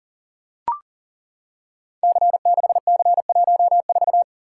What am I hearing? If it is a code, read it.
Morse code, heard as a keyed tone, about 40 wpm, 690 Hz. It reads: C6C14
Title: Code Groups - 5 Letter:Numbers - 40wpm